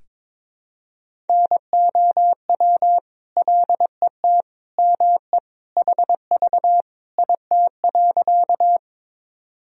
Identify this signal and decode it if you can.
Morse code, heard as a keyed tone, about 22 wpm, 700 Hz. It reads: NOW LET ME HV IT.
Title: Sentences from Top 300 Words and Abbreviations 22wpm